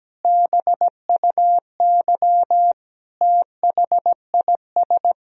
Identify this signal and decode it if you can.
Morse code, heard as a keyed tone, about 17 wpm, 700 Hz. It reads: BUY THIS
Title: Sentences from Top 1000 Words - Review Part 4 - 17wpm